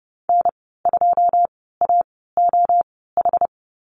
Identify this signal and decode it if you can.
Morse code, heard as a keyed tone, about 30 wpm, spaced as for 17 wpm, 700 Hz. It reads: N2AOH
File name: Lesson 15g - Number 5 - Call Signs 30wpm